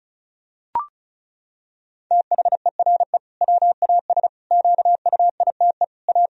Morse code, heard as a keyed tone, about 36 wpm, 700 Hz. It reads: THERE WAS QUITE A
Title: Sentences from Top 1000 Words - Review Part 1 - 35wpm